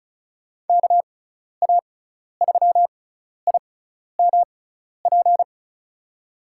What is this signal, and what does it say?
Morse code, heard as a keyed tone, about 36 wpm, spaced as for 12 wpm, 690 Hz. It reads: KA3IMP
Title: USA Call Signs - 6 Characters - 6x Character Spacing - 35wpm